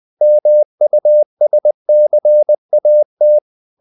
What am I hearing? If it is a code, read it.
Morse code, heard as a keyed tone, about 20 wpm, 600 Hz. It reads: MUSCAT
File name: CW / Practice / NationalCapitals20